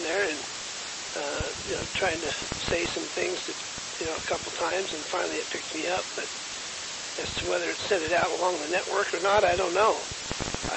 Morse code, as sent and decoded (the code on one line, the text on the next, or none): none